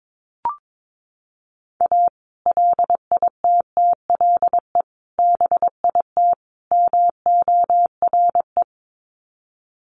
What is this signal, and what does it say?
Morse code, heard as a keyed tone, about 22 wpm, 700 Hz. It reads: A LITTLE BIT MORE
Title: Sets of 4 Words 22wpm